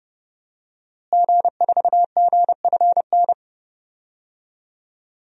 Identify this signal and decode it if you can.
Morse code, heard as a keyed tone, about 30 wpm, 700 Hz. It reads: G4GFN
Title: DXCC-100 Call Signs - 5 Characters - Repeated 3x - 30wpm